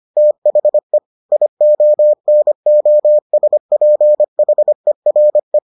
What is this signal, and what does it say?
Morse code, heard as a keyed tone, about 25 wpm, 600 Hz. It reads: THE IONOSPHERE